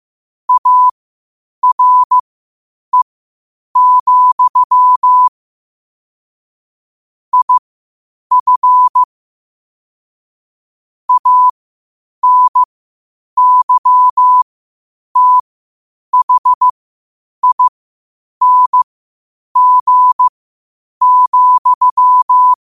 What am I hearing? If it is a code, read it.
Morse code, heard as a keyed tone, about 15 wpm, spaced as for 8 wpm, 1000 Hz. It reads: ARE, IF ANYTHING,
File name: qotdfiles / 8wpm